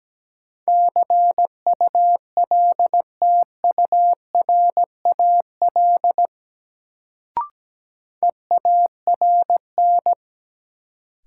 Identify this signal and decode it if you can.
Morse code, heard as a keyed tone, about 17 wpm, 700 Hz. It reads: CULTURAL EARN
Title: Sentences from Top 1500 Words - New Word Drill - 17wpm